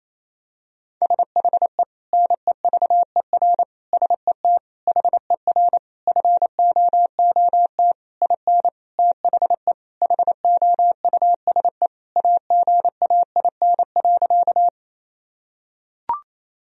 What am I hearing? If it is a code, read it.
Morse code, heard as a keyed tone, about 28 wpm, 700 Hz. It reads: SHE NEVER SET HER FOOT IN THE HOUSE AGAIN.
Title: Sentences from Top 1000 Words - Review Part 3 - 28wpm